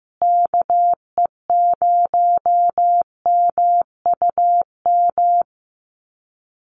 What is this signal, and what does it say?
Morse code, heard as a keyed tone, about 15 wpm, 700 Hz. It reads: KE0MUM
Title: USA Call Signs Repeated 3x 15wpm